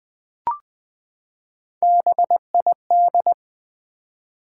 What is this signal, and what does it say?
Morse code, heard as a keyed tone, about 20 wpm, 700 Hz. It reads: BID
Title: Three Letter Words - 20wpm